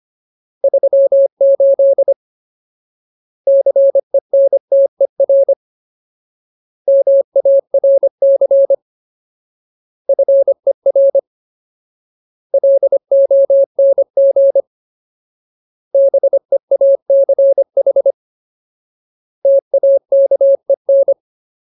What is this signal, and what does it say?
Morse code, heard as a keyed tone, about 25 wpm, 550 Hz. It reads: 38 CENTER MARC FER LONG BEACH TAKEN
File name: Ultimate Word List / 25WPM Ultimate